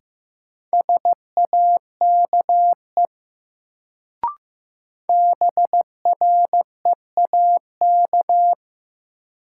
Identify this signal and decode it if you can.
Morse code, heard as a keyed tone, about 15 wpm, 700 Hz. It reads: SAKE BREAK
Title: Sets of 4 Words - New Word Drill - 15wpm